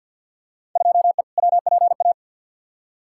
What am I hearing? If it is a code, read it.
Morse code, heard as a keyed tone, about 47 wpm, 700 Hz. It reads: 2E WPA
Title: Field Day Exchanges Encore - 50wpm